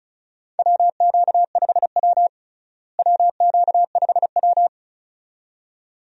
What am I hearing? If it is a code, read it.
Morse code, heard as a keyed tone, about 36 wpm, 700 Hz. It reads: WQ5W WQ5W
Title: USA Call Signs - 4 Characters - Repeated 2x - 35wpm